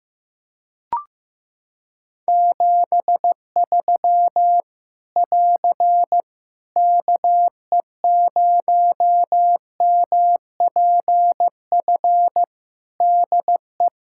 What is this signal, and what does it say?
Morse code, heard as a keyed tone, about 15 wpm, 700 Hz. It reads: 73 <AR> KE0MPF DE
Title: Practice QSO - Exchange 4 - Ending - 15wpm